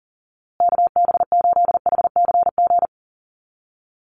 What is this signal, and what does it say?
Morse code, heard as a keyed tone, about 37 wpm, 700 Hz. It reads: KB8HCG
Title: USA Call Signs Repeated 2x 40wpm